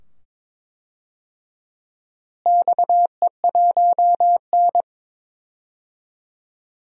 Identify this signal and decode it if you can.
Morse code, heard as a keyed tone, about 21 wpm, 700 Hz. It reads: XE1N
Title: DXCC-100 Call Signs - 4 Characters - Repeated 2x - 22wpm